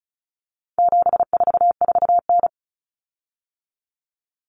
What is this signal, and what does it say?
Morse code, heard as a keyed tone, about 34 wpm, 700 Hz. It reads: 744N
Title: Common Cut Numbers - 4-Digit 35wpm